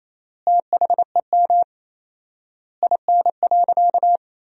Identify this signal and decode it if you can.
Morse code, heard as a keyed tone, about 28 wpm, 700 Hz. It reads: THEM IN.
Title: 4x Word-Spacing - Sentences from Top 500 Words - 28wpm